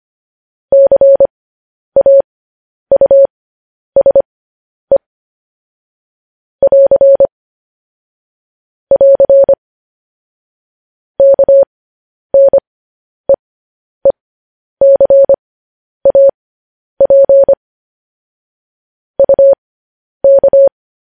Morse code, heard as a keyed tone, about 25 wpm, spaced as for 10 wpm, 550 Hz. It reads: CAUSE <AR> <AR> KNEECAP UK
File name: podcast / News headlines-1x-25x10